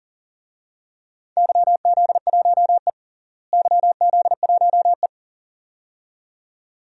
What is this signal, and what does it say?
Morse code, heard as a keyed tone, about 40 wpm, 700 Hz. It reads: YZ1E YZ1E
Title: DXCC-100 Call Signs - 4 Characters - Repeated 2x - 40wpm